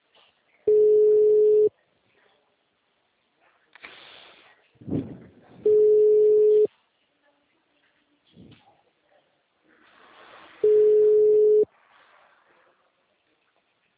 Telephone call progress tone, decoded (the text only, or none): ringback tone